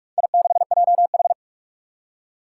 Morse code, heard as a keyed tone, about 44 wpm, 700 Hz. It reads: I6JH
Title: Four Challenging Letter-Number Combinations 45wpm